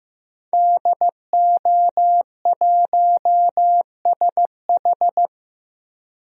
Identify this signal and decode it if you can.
Morse code, heard as a keyed tone, about 15 wpm, 700 Hz. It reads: DO1SH
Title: DXCC-100 Call Signs - 5 Characters - 15wpm